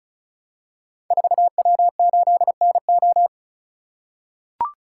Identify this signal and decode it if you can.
Morse code, heard as a keyed tone, about 36 wpm, 700 Hz. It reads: 4W8NO
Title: Code Groups - 5 Letter:Numbers - 35wpm